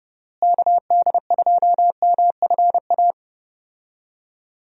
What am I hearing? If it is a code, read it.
Morse code, heard as a keyed tone, about 32 wpm, 700 Hz. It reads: KD2MFA
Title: USA Call Signs - 6 Characters - Repeated 3x - 30wpm